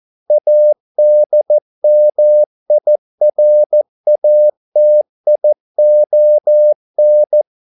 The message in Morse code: ADMIRATION

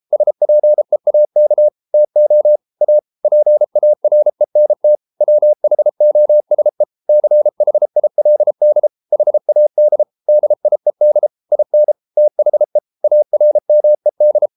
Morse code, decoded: SPEAK TO A PARENT WHOSE CHILD HAD DIED IN THE ARMED

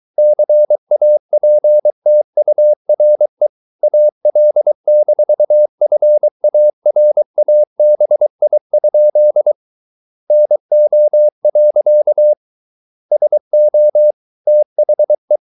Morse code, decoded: CAPTURE AL-FARABI? NO. SO THE

